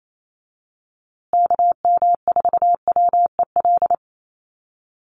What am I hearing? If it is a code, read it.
Morse code, heard as a keyed tone, about 28 wpm, 700 Hz. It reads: KM4WEL
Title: USA Call Signs Repeated 2x 28wpm